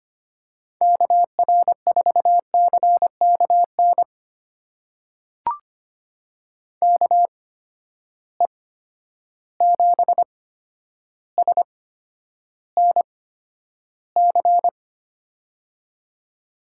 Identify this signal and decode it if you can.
Morse code, heard as a keyed tone, about 25 wpm, spaced as for 11 wpm, 700 Hz. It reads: KR4CKN KE7SNC
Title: USA Call Signs - 8x Character Spacing - 25wpm